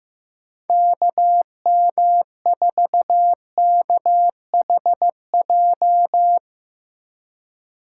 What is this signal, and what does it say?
Morse code, heard as a keyed tone, about 15 wpm, 700 Hz. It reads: KM4KHJ
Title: USA Call Signs - 6 Characters - 3x Character Spacing - 15wpm